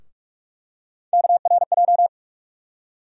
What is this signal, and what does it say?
Morse code, heard as a keyed tone, about 46 wpm, 700 Hz. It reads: KRJ
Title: USA Call Sign Postfixes - 3 Letters 45wpm